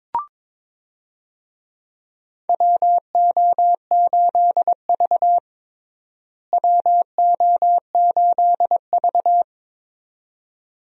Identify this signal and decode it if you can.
Morse code, heard as a keyed tone, about 22 wpm, 700 Hz. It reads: WO8V WO8V
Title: USA Call Signs - 4 Characters - Repeated 2x - 22wpm